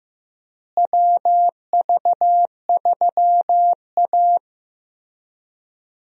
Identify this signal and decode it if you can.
Morse code, heard as a keyed tone, about 15 wpm, 700 Hz. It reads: WV3A
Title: USA Call Signs - 4 Characters - Repeated 3x - 15wpm